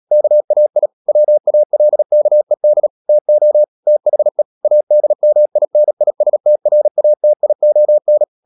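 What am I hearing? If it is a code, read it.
Morse code, heard as a keyed tone, about 37 wpm, 600 Hz. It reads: KAI WALKED TO THE ADMINISTRATION
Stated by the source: Morse code practice at 35–39 wpm sending English text of Ken Follett's novel Never, from CW med Gnister.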